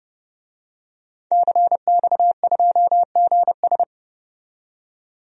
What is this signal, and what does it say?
Morse code, heard as a keyed tone, about 29 wpm, 700 Hz. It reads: CX2GS
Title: DXCC-100 Call Signs - 5 Characters - 30wpm